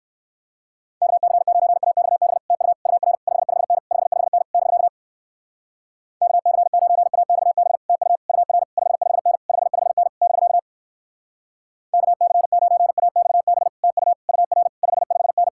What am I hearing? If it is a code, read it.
Morse code, heard as a keyed tone, about 64 wpm, 700 Hz. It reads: KC9AXB TU UR 55N 55N <BK> KC9AXB TU UR 55N 55N <BK> KC9AXB TU UR 55N